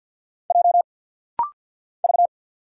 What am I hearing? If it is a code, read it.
Morse code, heard as a keyed tone, about 48 wpm, 700 Hz. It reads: J V
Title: Single Letter-Number - Rapid-Fire 50wpm